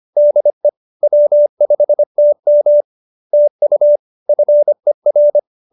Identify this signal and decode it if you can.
Morse code, heard as a keyed tone, about 25 wpm, 590 Hz. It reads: DE W5TM TU FER